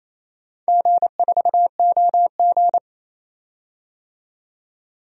Morse code, heard as a keyed tone, about 28 wpm, 700 Hz. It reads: G4OG
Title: DXCC-100 Call Signs - 4 Characters - 28wpm